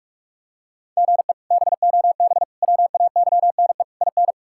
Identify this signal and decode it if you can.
Morse code, heard as a keyed tone, about 46 wpm, 700 Hz. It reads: GE BOB WAYNE IN
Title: K1USN SST QSO Practice - 45wpm